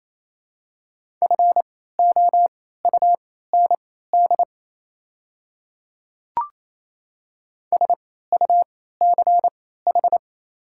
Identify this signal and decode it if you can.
Morse code, heard as a keyed tone, about 28 wpm, spaced as for 16 wpm, 700 Hz. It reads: FOUND SUCH
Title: 3x Character Spacing - Top 300 Words - 28wpm